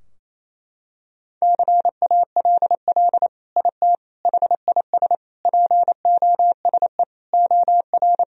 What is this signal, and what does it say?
Morse code, heard as a keyed tone, about 27 wpm, 700 Hz. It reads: CALL IT HIS POSE OR
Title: Sentences from Top 1000 Words - Review Part 2 - 28wpm